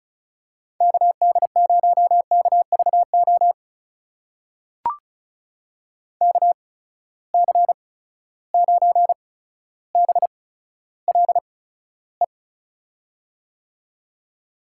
Morse code, heard as a keyed tone, about 36 wpm, spaced as for 15 wpm, 700 Hz. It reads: KD0KVO KC9BLE
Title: USA Call Signs - 8x Character Spacing - 35wpm